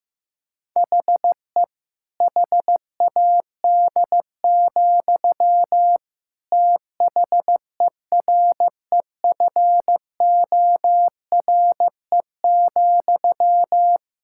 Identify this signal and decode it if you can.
Morse code, heard as a keyed tone, about 16 wpm, 700 Hz. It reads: HE HAD, THEREFORE,
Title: Sentences from Top 1000 Words - Review Part 5 - 15wpm